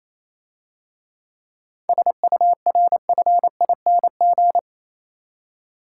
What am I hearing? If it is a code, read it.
Morse code, heard as a keyed tone, about 27 wpm, 700 Hz. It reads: SURFING
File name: Easy Words - Speed-Racing with Hints - 22wpm